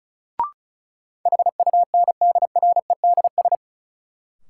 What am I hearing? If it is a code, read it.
Morse code, heard as a keyed tone, about 34 wpm, 700 Hz. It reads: HUNDREDS
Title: Sets of 4 Words - New Word Drill - 35wpm